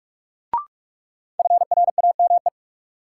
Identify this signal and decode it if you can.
Morse code, heard as a keyed tone, about 46 wpm, 700 Hz. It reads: FRAME
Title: Sentences from Top 1000 Words - Review Part 6 - New Word Drill - 45wpm